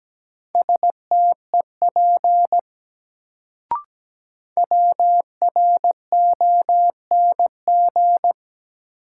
Morse code, heard as a keyed tone, about 17 wpm, 700 Hz. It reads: STEP WRONG
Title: Sets of 5 Words Encore - New Word Drill - 17wpm